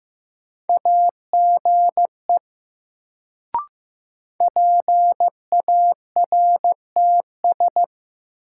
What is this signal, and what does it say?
Morse code, heard as a keyed tone, about 16 wpm, 700 Hz. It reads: AGE PARTS
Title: Sets of 3 Words - New Word Drill - 15wpm